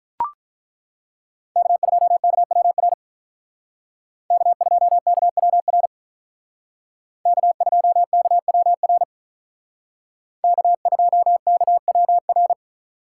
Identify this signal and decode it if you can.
Morse code, heard as a keyed tone, about 42 wpm, 700 Hz. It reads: K2KWR K2KWR K2KWR K2KWR
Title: Speed Racing - USA 1x3 Call Signs 35wpm